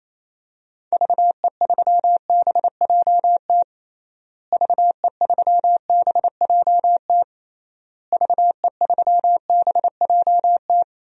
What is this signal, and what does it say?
Morse code, heard as a keyed tone, about 28 wpm, 700 Hz. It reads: VE3BJT VE3BJT VE3BJT
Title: DXCC-100 Call Signs - 6 Characters - Repeated 3x - 28wpm